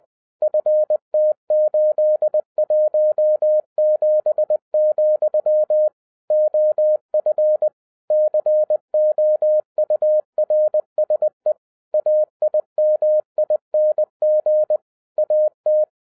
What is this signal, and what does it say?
Morse code, heard as a keyed tone, about 20 wpm, 610 Hz. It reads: FT817, OF COURSE AIMING AT
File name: 2012-01-mp3 / n05-CAME BACK TO HAM RADIO AFTER 30 YEARS OF INTERVAL-20wpm